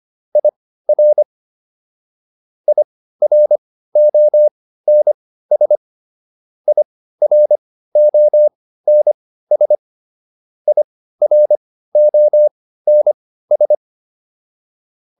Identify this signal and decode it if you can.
Morse code, heard as a keyed tone, about 25 wpm, spaced as for 15 wpm, 600 Hz. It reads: IR IRONS IRONS IRONS